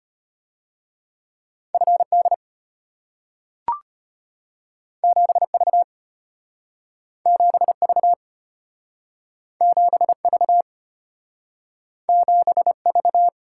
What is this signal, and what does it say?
Morse code, heard as a keyed tone, about 34 wpm, 700 Hz. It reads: FD 7V 7V 7V 7V